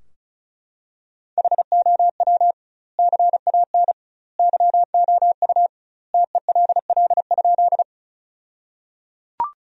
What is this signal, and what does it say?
Morse code, heard as a keyed tone, about 34 wpm, 700 Hz. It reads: HOW CAN YOU TELL?
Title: 2x Word-Spacing - Sentences from Top 200 Words - 35wpm